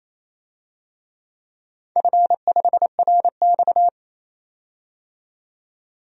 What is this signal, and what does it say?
Morse code, heard as a keyed tone, about 27 wpm, 700 Hz. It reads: F5RX
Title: DXCC-100 Call Signs - 4 Characters - 28wpm